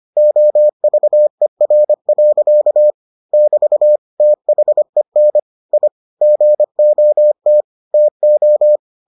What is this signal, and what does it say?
Morse code, heard as a keyed tone, about 25 wpm, 600 Hz. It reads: OVER. <BT> THEN I GOT TO